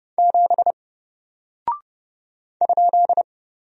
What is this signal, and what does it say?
Morse code, heard as a keyed tone, about 32 wpm, 700 Hz. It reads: 7 ?
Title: Single Character (Letter:Number:Punctuation:Prosign) 30wpm